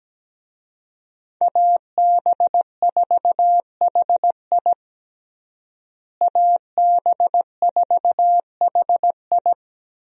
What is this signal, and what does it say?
Morse code, heard as a keyed tone, about 17 wpm, 700 Hz. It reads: AB4HI AB4HI
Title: USA Call Signs - 5 Characters - Repeated 3x - 17wpm